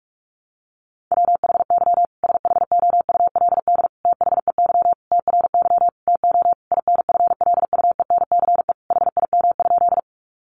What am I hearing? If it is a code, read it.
Morse code, heard as a keyed tone, about 44 wpm, 700 Hz. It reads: WHY SHOULD THEY TRY TO INFLUENCE HIM?
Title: Sentences from Top 1100 Words 45wpm